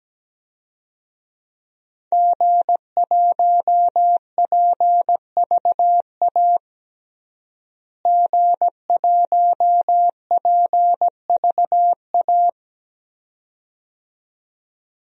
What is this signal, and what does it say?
Morse code, heard as a keyed tone, about 17 wpm, 700 Hz. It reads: G1PVA G1PVA
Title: DXCC-100 Call Signs - 5 Characters - Repeated 2x - 17wpm